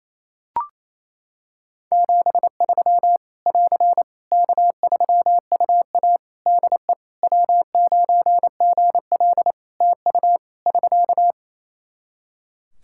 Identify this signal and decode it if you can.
Morse code, heard as a keyed tone, about 28 wpm, 700 Hz. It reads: 73 <AR> K3UA DE W9GL TU <SK>
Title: Practice QSO - Exchange 4 - Ending - 28wpm